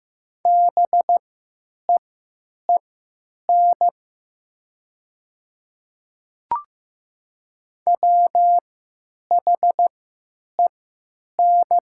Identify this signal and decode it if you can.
Morse code, heard as a keyed tone, about 15 wpm, spaced as for 9 wpm, 700 Hz. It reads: BEEN WHEN
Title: 3x Character Spacing - Top 300 Words - 15wpm